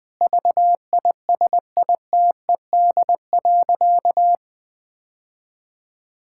Morse code, heard as a keyed tone, about 21 wpm, 700 Hz. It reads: VISITED.
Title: Sentences with 2-Word Phrases - 20wpm